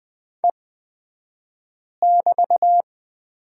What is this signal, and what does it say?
Morse code, heard as a keyed tone, about 20 wpm, spaced as for 5 wpm, 700 Hz. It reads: E<BT>
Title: 228 Lesson 37d - Letter x - Two Characters 20wpm